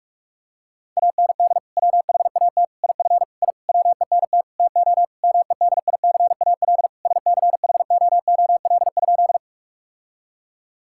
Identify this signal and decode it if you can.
Morse code, heard as a keyed tone, about 46 wpm, 700 Hz. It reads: AND WHAT IF I WENT TO MEDICAL SCHOOL?
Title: Sentences from Top 1000 Words - Review Part 5 - 45wpm